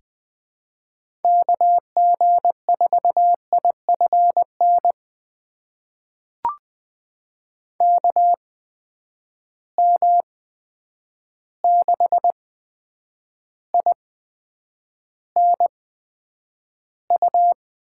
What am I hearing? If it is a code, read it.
Morse code, heard as a keyed tone, about 20 wpm, spaced as for 9 wpm, 700 Hz. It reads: KG4IFN KM6INU
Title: USA Call Signs - 6 Characters - 8x Character Spacing - 20wpm